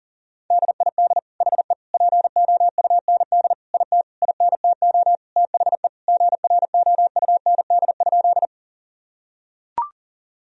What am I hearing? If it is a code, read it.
Morse code, heard as a keyed tone, about 40 wpm, 700 Hz. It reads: DID HE POUND IT INTO THE GROUND?
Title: Sentences from Top 1000 Words - Review Part 2 - 40wpm